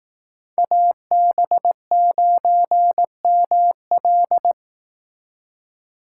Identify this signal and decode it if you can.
Morse code, heard as a keyed tone, about 18 wpm, 700 Hz. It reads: AB9ML
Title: Speed Racing - USA Call Signs 15wpm